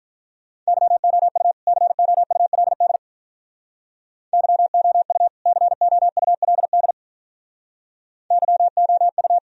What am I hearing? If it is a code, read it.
Morse code, heard as a keyed tone, about 46 wpm, 700 Hz. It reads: YOU COULD YOU COULD YOU